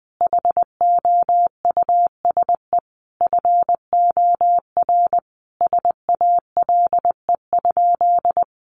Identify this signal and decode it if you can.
Morse code, heard as a keyed tone, about 19 wpm, 700 Hz. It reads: HOUSE FOR SALE?
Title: Sentences from Top 1400 Words 20wpm